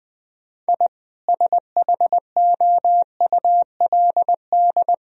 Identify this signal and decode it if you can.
Morse code, heard as a keyed tone, about 20 wpm, 700 Hz. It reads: I SHOULD
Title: Sentences from Top 1000 Words - Review Part 1 - 20wpm